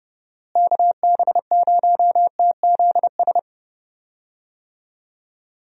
Morse code, heard as a keyed tone, about 30 wpm, 700 Hz. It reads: KB0TZS
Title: USA Call Signs - 6 Characters - Repeated 3x - 30wpm